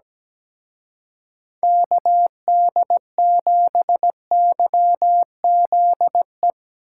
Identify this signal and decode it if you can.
Morse code, heard as a keyed tone, about 18 wpm, 700 Hz. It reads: KD7YZE
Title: Speed Racing - USA Call Signs 17wpm